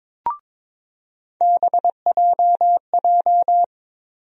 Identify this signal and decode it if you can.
Morse code, heard as a keyed tone, about 22 wpm, 700 Hz. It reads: BJJ
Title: USA Call Sign Postfixes - 3 Letters 22wpm